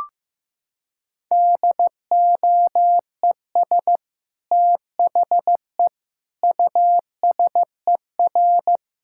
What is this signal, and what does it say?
Morse code, heard as a keyed tone, about 15 wpm, 700 Hz. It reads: DOES THE USER
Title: Sentences from Top 1000 Words - Review Part 5 - 15wpm